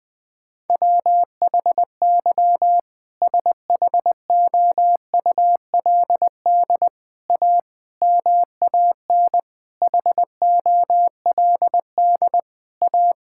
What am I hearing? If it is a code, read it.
Morse code, heard as a keyed tone, about 21 wpm, 700 Hz. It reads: WHY SHOULD A MAN HOLD A
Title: Sentences from Top 1600 Words 20wpm